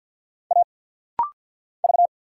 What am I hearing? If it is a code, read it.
Morse code, heard as a keyed tone, about 50 wpm, 700 Hz. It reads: A V
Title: Single Letter-Number Alt - Rapid-Fire 50wpm